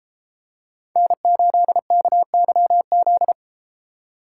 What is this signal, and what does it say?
Morse code, heard as a keyed tone, about 32 wpm, 700 Hz. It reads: N8KYZ